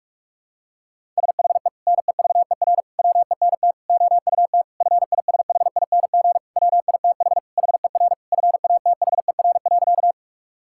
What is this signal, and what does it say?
Morse code, heard as a keyed tone, about 46 wpm, 700 Hz. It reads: SHE NEVER WENT OUT FISHING WITH HER FATHER.
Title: Sentences from Top 1000 Words - Review Part 5 - 45wpm